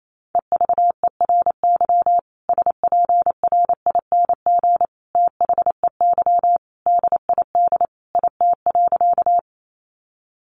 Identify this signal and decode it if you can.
Morse code, heard as a keyed tone, about 28 wpm, 700 Hz. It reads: EVERY SPRING THEY DID IT.
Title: Sentences from Top 1200 Words 28wpm